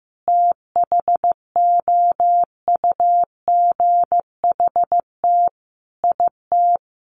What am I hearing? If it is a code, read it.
Morse code, heard as a keyed tone, about 15 wpm, 700 Hz. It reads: THOUGHT IT
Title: Sets of 5 Words 15wpm